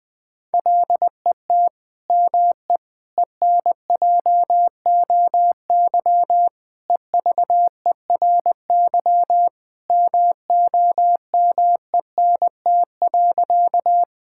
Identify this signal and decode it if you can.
Morse code, heard as a keyed tone, about 21 wpm, 700 Hz. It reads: LET ME ENJOY EVERY MOMENT.
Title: Sentences from Top 1000 Words - Review Part 5 - 20wpm